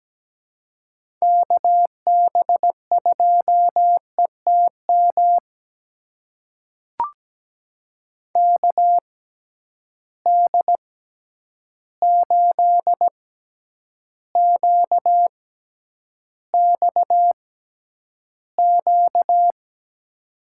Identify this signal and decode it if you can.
Morse code, heard as a keyed tone, about 17 wpm, spaced as for 9 wpm, 700 Hz. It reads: KB2ETM KD8QXQ
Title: USA Call Signs - 6 Characters - 6x Character Spacing - 17wpm